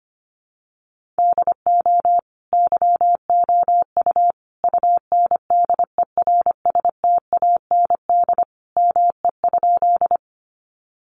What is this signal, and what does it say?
Morse code, heard as a keyed tone, about 25 wpm, 700 Hz. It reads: DO YOU UNDERSTAND ME?
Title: Sentences from Top 600 Words 25wpm